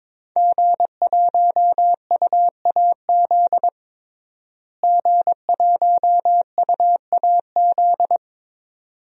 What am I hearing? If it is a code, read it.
Morse code, heard as a keyed tone, about 21 wpm, 700 Hz. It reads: G1UAZ G1UAZ
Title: DXCC-100 Call Signs - 5 Characters - Repeated 2x - 22wpm